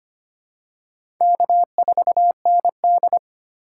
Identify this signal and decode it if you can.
Morse code, heard as a keyed tone, about 25 wpm, 700 Hz. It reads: K4ND
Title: USA Call Signs - 4 Characters - 3x Character Spacing - 25wpm